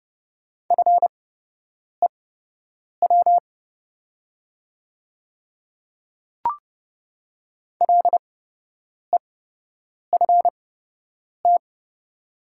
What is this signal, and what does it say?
Morse code, heard as a keyed tone, about 29 wpm, spaced as for 8 wpm, 690 Hz. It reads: FEW LEFT
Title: 8x Character Spacing - Top 300 Words - 30wpm